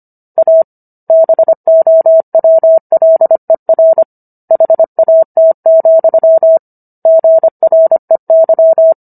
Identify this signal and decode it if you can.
Morse code, heard as a keyed tone, about 25 wpm, 650 Hz. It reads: A BOWLER HAT, GREY